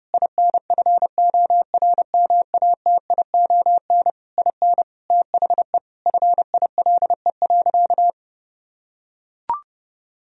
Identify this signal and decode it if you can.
Morse code, heard as a keyed tone, about 30 wpm, 700 Hz. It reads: INFORMATION IN THE FILE.
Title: Sentences from Top 1000 Words - Review Part 4 - 30wpm